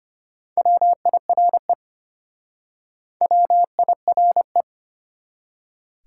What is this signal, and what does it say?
Morse code, heard as a keyed tone, about 33 wpm, 700 Hz. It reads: WIRE WIRE